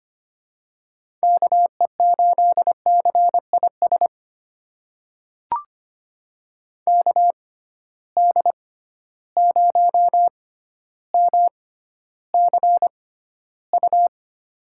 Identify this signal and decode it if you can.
Morse code, heard as a keyed tone, about 25 wpm, spaced as for 13 wpm, 700 Hz. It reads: KE8CIS KD0MCU
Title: USA Call Signs - 6 Characters - 6x Character Spacing - 25wpm